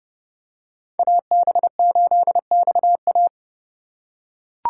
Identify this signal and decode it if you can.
Morse code, heard as a keyed tone, about 30 wpm, 700 Hz. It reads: AB8XA
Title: USA Call Signs - 5 Characters - 2x Character Spacing - 30wpm